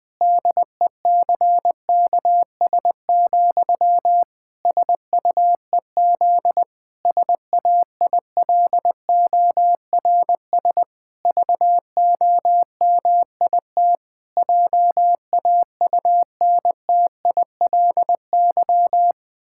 Morse code, heard as a keyed tone, about 21 wpm, 700 Hz. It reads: DECKS, SUEZ SAILORS VOMIT JAUNTILY
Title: Pangram Challenge v2 - 20wpm